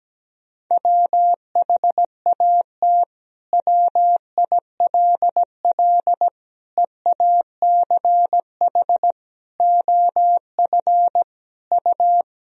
Morse code, heard as a keyed tone, about 17 wpm, 700 Hz. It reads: WHAT WILL EACH OF U
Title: Sentences from Top 200 Words and Abbreviations 17wpm